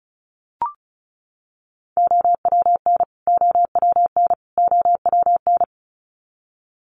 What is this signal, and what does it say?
Morse code, heard as a keyed tone, about 34 wpm, 700 Hz. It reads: OWN OWN OWN
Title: Top 500 Words Repeated 35wpm